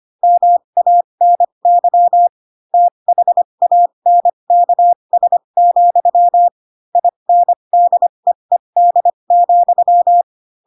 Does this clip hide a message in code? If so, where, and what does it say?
Yes: Morse code from 0.2 to 10.2 s, heard as a keyed tone, about 25 wpm, spaced as for 22 wpm, 700 Hz. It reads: MANY THANKS, INDEED,